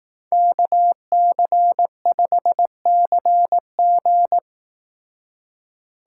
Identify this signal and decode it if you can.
Morse code, heard as a keyed tone, about 18 wpm, 700 Hz. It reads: KC5CG
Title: Speed Racing - USA 2x2 Call Signs 15wpm